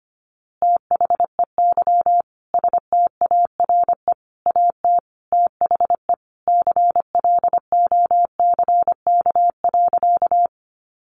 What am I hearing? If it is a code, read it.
Morse code, heard as a keyed tone, about 25 wpm, 700 Hz. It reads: THEY STARE AT THE CLOCK.
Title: Easy Phrases 25wpm